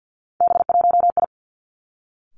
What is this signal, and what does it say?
Morse code, heard as a keyed tone, about 50 wpm, 700 Hz. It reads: B1I
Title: Three Challenging Letter-Number Combinations 50wpm